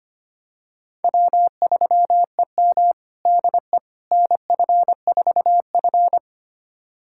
Morse code, heard as a keyed tone, about 25 wpm, 700 Hz. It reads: W3EM DE NF4F
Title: USA Callsign DE Callsign - 4 Characters - 25wpm